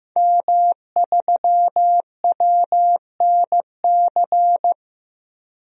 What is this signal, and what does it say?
Morse code, heard as a keyed tone, about 15 wpm, 700 Hz. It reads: M3WNC
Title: DXCC-100 Call Signs - 5 Characters - Repeated 2x - 15wpm